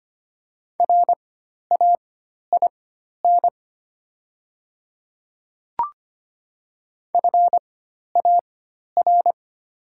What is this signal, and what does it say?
Morse code, heard as a keyed tone, about 25 wpm, spaced as for 12 wpm, 700 Hz. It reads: RAIN FAR
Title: 4x Character Spacing - Top 300 Words - 25wpm